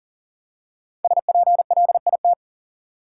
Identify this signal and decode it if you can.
Morse code, heard as a keyed tone, about 40 wpm, 700 Hz. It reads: SPLIT
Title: Five Letter Words - 40wpm